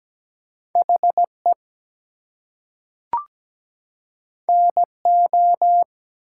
Morse code, heard as a keyed tone, about 17 wpm, 700 Hz. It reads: HE NO
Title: Two Letter Words - 17wpm